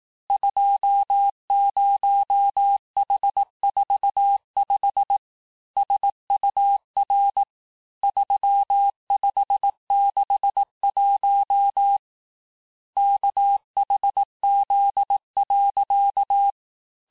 Morse code, heard as a keyed tone, about 18 wpm, 800 Hz. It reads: 20H45 SUR 3561 KHZ.
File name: cours-du-lundi files / cours-1852-texte-chiffre-vitesse-17MPM